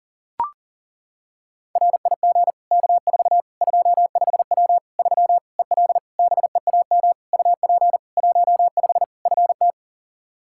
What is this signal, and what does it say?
Morse code, heard as a keyed tone, about 40 wpm, 700 Hz. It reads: RIG K4 25W 3 EL BEAM UP 15 FT
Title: Practice QSO - Exchange 2 - Info - 40wpm